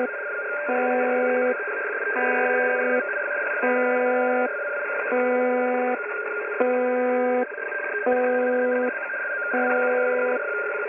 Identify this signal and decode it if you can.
Two-tone teleprinter signal, about 170 Hz shift.
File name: uploader / websdr recording 201